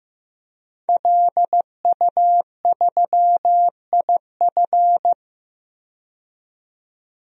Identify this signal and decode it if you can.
Morse code, heard as a keyed tone, about 15 wpm, 700 Hz. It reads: LU3IF
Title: DXCC-100 Call Signs - 5 Characters - 15wpm